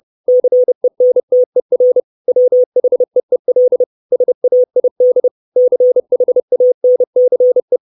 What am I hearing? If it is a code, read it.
Morse code, heard as a keyed tone, about 30 wpm, 500 Hz. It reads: CENTER WHEEL SAID CHANCE
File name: 632 Words 30WPM